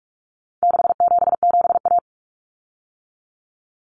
Morse code, heard as a keyed tone, about 44 wpm, 700 Hz. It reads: 677A
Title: Common Cut Numbers - 4-Digit 45wpm